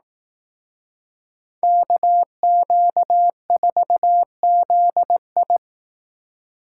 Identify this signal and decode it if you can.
Morse code, heard as a keyed tone, about 19 wpm, 700 Hz. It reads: KQ4ZI
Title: Speed Racing - USA 2x2 Call Signs 15wpm